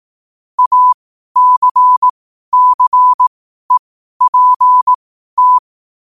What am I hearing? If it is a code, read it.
Morse code, heard as a keyed tone, about 18 wpm, spaced as for 12 wpm, 1000 Hz. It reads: ACCEPT
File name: qotdfiles / 12wpm